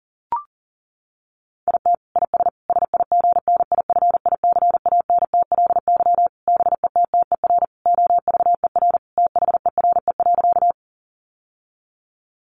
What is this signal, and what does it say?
Morse code, heard as a keyed tone, about 40 wpm, 700 Hz. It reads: IT IS SIGNIFICANTLY BETTER OVER THERE.
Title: Sentences from Top 1500 Words 40wpm